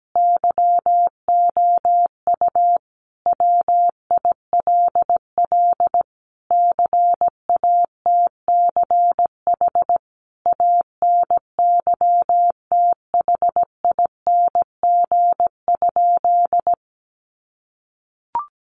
Morse code, heard as a keyed tone, about 17 wpm, 700 Hz. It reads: YOU WILL CATCH ANYTHING?
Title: Sentences from Top 1500 Words 17wpm